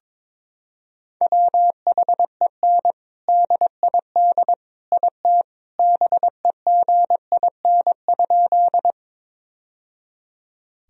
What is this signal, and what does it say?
Morse code, heard as a keyed tone, about 22 wpm, 700 Hz. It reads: WHEN DID IT BEGIN?
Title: Sentences from Top 1000 Words - Review Part 1 - 22wpm